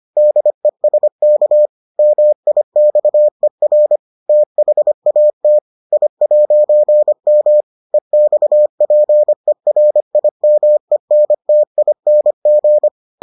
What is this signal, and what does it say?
Morse code, heard as a keyed tone, about 25 wpm, 600 Hz. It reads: DESK MIXER THAT I'M EXPERIMENTING